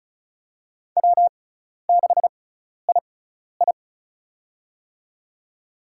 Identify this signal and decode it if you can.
Morse code, heard as a keyed tone, about 36 wpm, spaced as for 12 wpm, 700 Hz. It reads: W6II
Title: USA Call Signs - 6x Character Spacing - 35wpm